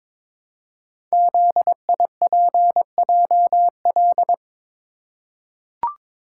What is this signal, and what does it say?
Morse code, heard as a keyed tone, about 22 wpm, 700 Hz. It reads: ZIPJL
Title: Code Groups - 5 Letter:Numbers - 22wpm